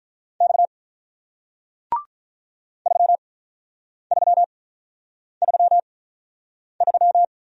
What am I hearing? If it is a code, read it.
Morse code, heard as a keyed tone, about 48 wpm, spaced as for 9 wpm, 700 Hz. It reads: X 3333